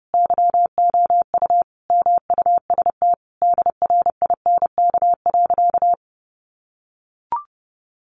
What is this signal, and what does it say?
Morse code, heard as a keyed tone, about 30 wpm, 700 Hz. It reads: YOU MUST DRINK.
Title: Sentences from Top 1500 Words 30wpm